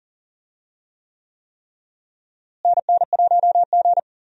Morse code, heard as a keyed tone, about 40 wpm, 700 Hz. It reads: NN1G
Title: USA Call Signs - 4 Characters - Repeated 3x - 40wpm